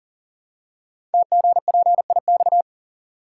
Morse code, heard as a keyed tone, about 41 wpm, 700 Hz. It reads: TGPIX
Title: Code Groups - 5 Letters - 40wpm